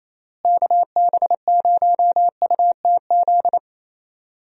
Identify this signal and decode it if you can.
Morse code, heard as a keyed tone, about 28 wpm, 700 Hz. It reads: KB0UTZ
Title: USA Call Signs - 6 Characters - Repeated 3x - 28wpm